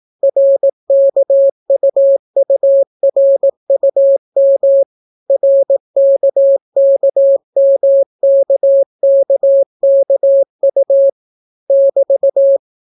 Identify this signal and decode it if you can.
Morse code, heard as a keyed tone, about 18 wpm, 550 Hz. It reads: RKUURUM RKKMKKKU <BT>